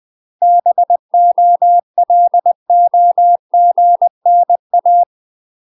Morse code, heard as a keyed tone, about 20 wpm, 700 Hz. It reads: BOLOGNA
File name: UK MASTER QSOs TRAINING/20WPM